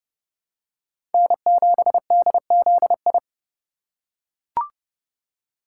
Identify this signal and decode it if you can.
Morse code, heard as a keyed tone, about 30 wpm, 700 Hz. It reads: N7DZI
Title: USA Call Signs - 2x Character Spacing - 30wpm